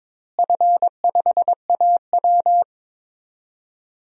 Morse code, heard as a keyed tone, about 22 wpm, 700 Hz. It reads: F5AW
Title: DXCC-100 Call Signs - 4 Characters - Repeated 2x - 22wpm